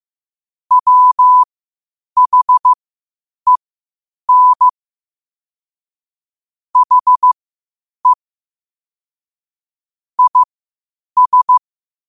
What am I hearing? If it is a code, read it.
Morse code, heard as a keyed tone, about 15 wpm, spaced as for 8 wpm, 1000 Hz. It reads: WHEN HE IS
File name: qotdfiles / 8wpm